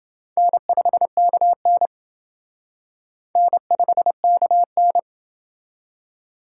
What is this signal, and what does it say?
Morse code, heard as a keyed tone, about 30 wpm, 700 Hz. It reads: N5KN N5KN